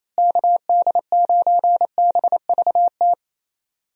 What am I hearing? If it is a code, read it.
Morse code, heard as a keyed tone, about 28 wpm, 700 Hz. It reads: KD9BVT
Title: USA Call Signs - 6 Characters - Repeated 2x - 28wpm